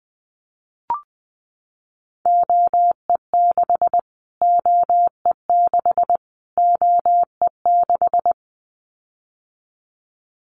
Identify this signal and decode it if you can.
Morse code, heard as a keyed tone, about 20 wpm, 700 Hz. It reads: OE6 OE6 OE6
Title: Callsign Prefixes for Top 25 DXCC - Repeated 3x - 20wpm